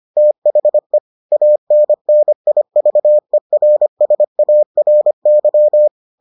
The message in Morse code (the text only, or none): THE ANNIVERSARY